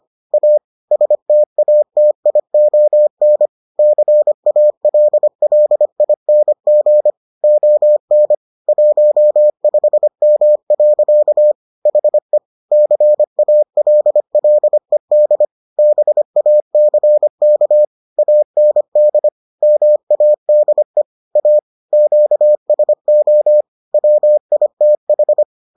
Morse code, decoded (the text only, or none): A STATION CALLING ON 15M. HE CALLED BACK AND MADE A QSO WITH